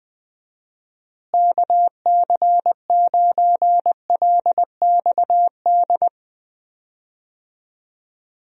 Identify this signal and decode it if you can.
Morse code, heard as a keyed tone, about 20 wpm, 700 Hz. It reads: KC9LXD
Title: USA Call Signs - 6 Characters - Repeated 3x - 20wpm